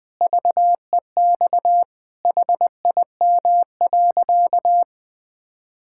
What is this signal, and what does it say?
Morse code, heard as a keyed tone, about 20 wpm, 700 Hz. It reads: VEX HIM.
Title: Pangram Challenge v2 - 20wpm